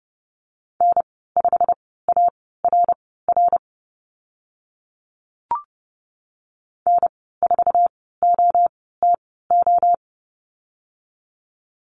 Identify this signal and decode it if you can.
Morse code, heard as a keyed tone, about 30 wpm, spaced as for 17 wpm, 700 Hz. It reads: N5ARR N4OTO
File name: Lesson 15g - Number 5 - Call Signs 30wpm